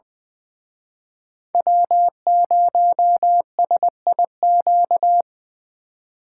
Morse code, heard as a keyed tone, about 24 wpm, 700 Hz. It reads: W0SIQ
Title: Speed Racing - USA 1x3 Call Signs 20wpm